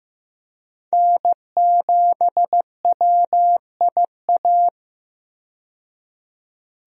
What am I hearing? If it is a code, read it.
Morse code, heard as a keyed tone, about 15 wpm, 700 Hz. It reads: N7WIA
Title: USA Call Signs - 5 Characters - 15wpm